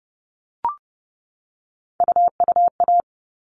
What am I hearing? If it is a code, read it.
Morse code, heard as a keyed tone, about 30 wpm, 700 Hz. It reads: UUA
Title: All Cut Numbers - 3-Digit 30wpm